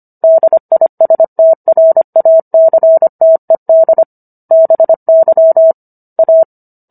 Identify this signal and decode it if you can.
Morse code, heard as a keyed tone, about 25 wpm, 650 Hz. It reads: DISTRACTED BY A